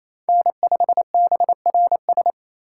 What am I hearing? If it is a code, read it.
Morse code, heard as a keyed tone, about 28 wpm, 700 Hz. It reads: N5BRS
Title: USA Call Signs - 5 Characters - 2x Character Spacing - 28wpm